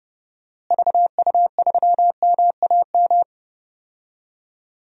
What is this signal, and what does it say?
Morse code, heard as a keyed tone, about 30 wpm, 700 Hz. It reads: VU3MAM
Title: DXCC-100 Call Signs - 6 Characters - Repeated 2x - 30wpm